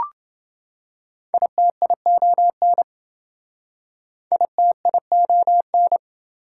Morse code, heard as a keyed tone, about 27 wpm, 700 Hz. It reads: ITION ITION